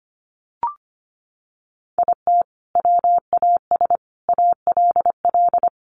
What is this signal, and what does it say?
Morse code, heard as a keyed tone, about 25 wpm, 700 Hz. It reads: IT WAS ALL
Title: Sentences from Top 600 Words 25wpm